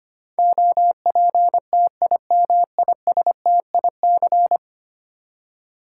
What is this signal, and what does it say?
Morse code, heard as a keyed tone, about 25 wpm, 700 Hz. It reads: OPTIMISTIC
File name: Ten Letter Words - 25wpm